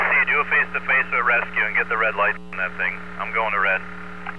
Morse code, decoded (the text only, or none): none